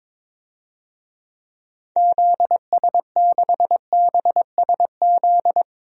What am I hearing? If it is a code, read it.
Morse code, heard as a keyed tone, about 22 wpm, 700 Hz. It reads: ZS6BSZ
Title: DXCC-100 Call Signs - 6 Characters - 22wpm